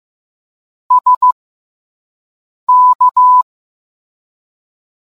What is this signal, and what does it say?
Morse code, heard as a keyed tone, about 15 wpm, spaced as for 5 wpm, 1000 Hz. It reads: SK